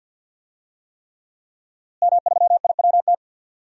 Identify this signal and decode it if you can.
Morse code, heard as a keyed tone, about 46 wpm, 700 Hz. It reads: M3IWT
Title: DXCC-100 Call Signs - 5 Characters - Repeated 3x - 50wpm